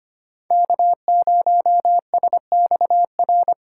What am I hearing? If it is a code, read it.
Morse code, heard as a keyed tone, about 25 wpm, 700 Hz. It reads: K0SXR
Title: USA Call Signs - 5 Characters - 3x Character Spacing - 25wpm